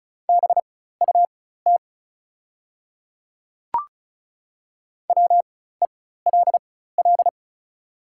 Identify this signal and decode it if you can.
Morse code, heard as a keyed tone, about 36 wpm, spaced as for 16 wpm, 700 Hz. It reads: BUT WELL
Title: 4x Character Spacing - Top 300 Words - 35wpm